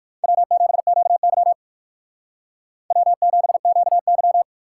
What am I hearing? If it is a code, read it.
Morse code, heard as a keyed tone, about 58 wpm, 700 Hz. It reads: W7QY W7QY